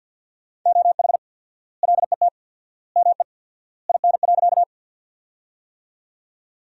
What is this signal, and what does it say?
Morse code, heard as a keyed tone, about 53 wpm, 700 Hz. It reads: OH LET ME IN.
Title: 4x Word-Spacing - Sentences from Top 500 Words - 50wpm